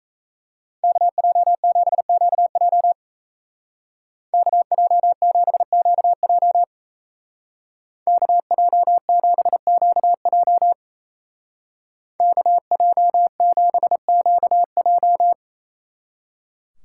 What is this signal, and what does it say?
Morse code, heard as a keyed tone, about 34 wpm, 700 Hz. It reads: KJ7QJ KJ7QJ KJ7QJ KJ7QJ
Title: Speed Racing - USA Call Signs 28wpm